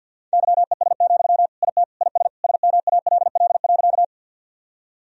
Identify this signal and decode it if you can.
Morse code, heard as a keyed tone, about 52 wpm, 700 Hz. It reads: YES, IT IS SMALL.
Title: Sentences from Top 1000 Words - Review Part 1 - 50wpm